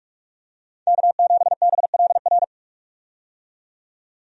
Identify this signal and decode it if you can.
Morse code, heard as a keyed tone, about 46 wpm, 700 Hz. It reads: K7BLR
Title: USA Call Signs - 5 Characters - 45wpm